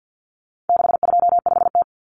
Morse code, heard as a keyed tone, about 50 wpm, 700 Hz. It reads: 625T
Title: Common Cut Numbers - 4-Digit 50wpm